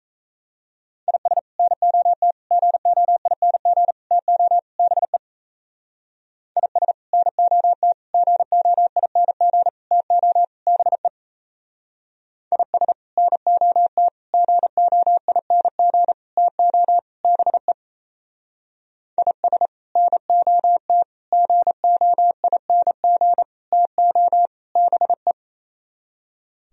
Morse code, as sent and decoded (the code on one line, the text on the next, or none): .. ... / -. --- - / --. --- .. -. --. / - --- / -... / .. ... / -. --- - / --. --- .. -. --. / - --- / -... / .. ... / -. --- - / --. --- .. -. --. / - --- / -... / .. ... / -. --- - / --. --- .. -. --. / - --- / -... .
IS NOT GOING TO BE IS NOT GOING TO BE IS NOT GOING TO BE IS NOT GOING TO BE